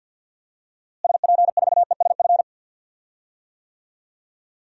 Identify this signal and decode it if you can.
Morse code, heard as a keyed tone, about 48 wpm, 700 Hz. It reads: SP4ESF
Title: DXCC-100 Call Signs - 6 Characters - Repeated 3x - 50wpm